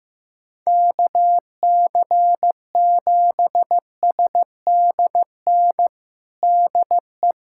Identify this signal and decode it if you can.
Morse code, heard as a keyed tone, about 15 wpm, 700 Hz. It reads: KC7SDN DE
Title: USA Callsign DE Callsign - 6 Characters - 15wpm